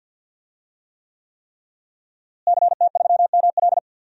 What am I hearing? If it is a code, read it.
Morse code, heard as a keyed tone, about 46 wpm, 700 Hz. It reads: CT3ML
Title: DXCC-100 Call Signs - 5 Characters - Repeated 3x - 50wpm